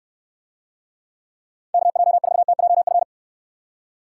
Morse code, heard as a keyed tone, about 69 wpm, 700 Hz. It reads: KJ4TJF